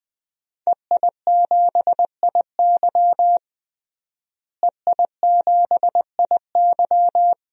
Morse code, heard as a keyed tone, about 21 wpm, 700 Hz. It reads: EI7IY EI7IY
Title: DXCC-100 Call Signs - 5 Characters - Repeated 3x - 20wpm